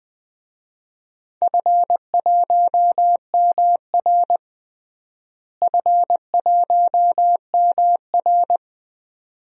Morse code, heard as a keyed tone, about 20 wpm, 700 Hz. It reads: F1MR F1MR
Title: DXCC-100 Call Signs - 4 Characters - Repeated 2x - 20wpm